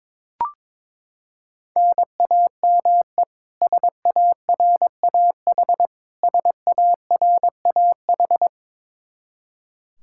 Morse code, heard as a keyed tone, about 22 wpm, 700 Hz. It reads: NAME SARAH SARAH
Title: Practice QSO - Exchange 1 - Beginning - 22wpm